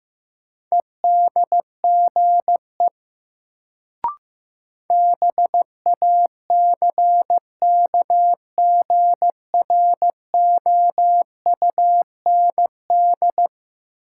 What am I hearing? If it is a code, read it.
Morse code, heard as a keyed tone, about 15 wpm, 700 Hz. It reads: EDGE BACKGROUND
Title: Sentences from Top 1300 Words - New Word Drill - 15wpm